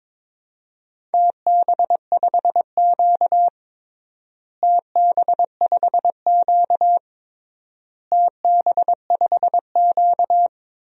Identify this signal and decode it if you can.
Morse code, heard as a keyed tone, about 22 wpm, 700 Hz. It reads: TB5Q TB5Q TB5Q
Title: DXCC-100 Call Signs - 4 Characters - Repeated 3x - 22wpm